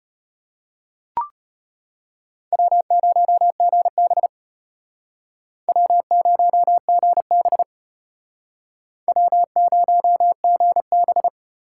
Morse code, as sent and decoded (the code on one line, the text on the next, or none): .-- ----- --. -... / .-- ----- --. -... / .-- ----- --. -...
W0GB W0GB W0GB